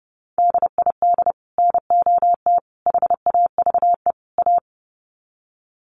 Morse code, heard as a keyed tone, about 30 wpm, 700 Hz. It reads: DID NOT HAVE A
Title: Sets of 4 Words 30wpm